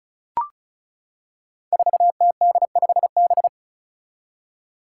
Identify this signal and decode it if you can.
Morse code, heard as a keyed tone, about 34 wpm, 700 Hz. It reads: 4TD5B
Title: Five Challenging Letter-Number Combinations 35wpm